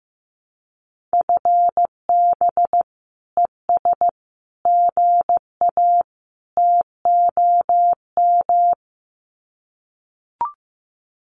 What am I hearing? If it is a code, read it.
Morse code, heard as a keyed tone, about 14 wpm, 700 Hz. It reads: FB ES GA TOM
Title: Common QSO Phrases Encore 15wpm